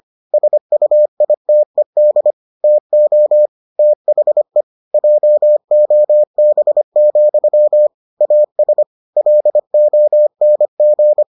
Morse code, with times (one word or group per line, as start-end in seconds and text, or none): 0.3-2.3 SUITED
2.6-3.5 TO
3.8-4.6 THE
4.9-7.9 JOB,
8.2-8.8 AS
9.2-11.2 LONG